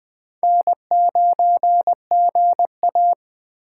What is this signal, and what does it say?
Morse code, heard as a keyed tone, about 23 wpm, 700 Hz. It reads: N9GA
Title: Speed Racing - USA 1x2 Call Signs 20wpm